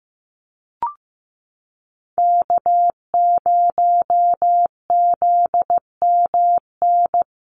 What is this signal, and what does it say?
Morse code, heard as a keyed tone, about 15 wpm, 700 Hz. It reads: K0ZMN
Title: USA Call Signs Repeated 2x 15wpm